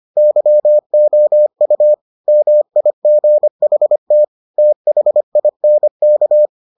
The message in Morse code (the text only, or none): YOU MIGHT THINK